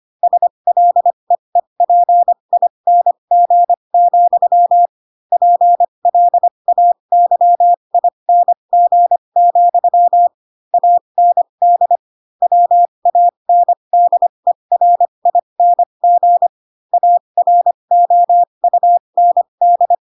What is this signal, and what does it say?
Morse code, heard as a keyed tone, about 25 wpm, spaced as for 22 wpm, 700 Hz. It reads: SLEEPING, PLAYING, AND WANDERING AROUND
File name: Pinocchio / Pinocchio ch4 18 25 ik1hge